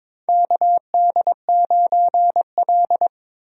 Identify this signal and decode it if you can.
Morse code, heard as a keyed tone, about 22 wpm, 700 Hz. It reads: KD9L
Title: USA Call Signs - 4 Characters - Repeated 3x - 22wpm